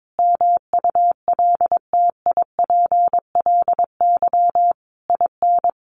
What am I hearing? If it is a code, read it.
Morse code, heard as a keyed tone, about 22 wpm, 700 Hz. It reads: MULTIPLY IN
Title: A Sound Of Thunder 22wpm